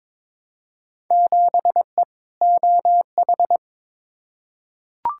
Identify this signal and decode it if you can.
Morse code, heard as a keyed tone, about 22 wpm, 700 Hz. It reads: 7E OH
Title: Field Day Exchanges Encore - 22wpm